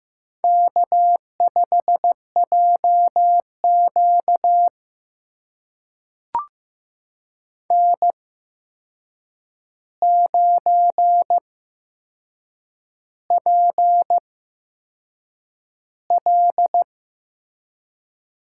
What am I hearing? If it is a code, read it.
Morse code, heard as a keyed tone, about 15 wpm, spaced as for 6 wpm, 700 Hz. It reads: K5JQ N9PL
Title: USA Call Signs - 4 Characters - 8x Character Spacing - 15wpm